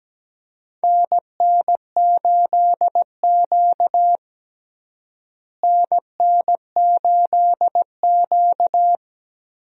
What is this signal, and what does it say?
Morse code, heard as a keyed tone, about 17 wpm, 700 Hz. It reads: NN8Q NN8Q
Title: USA Call Signs - 4 Characters - Repeated 2x - 17wpm